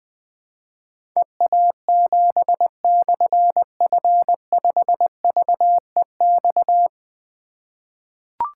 Morse code, heard as a keyed tone, about 20 wpm, 700 Hz. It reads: EA7/F5VEX
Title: Difficult Call Signs 20wpm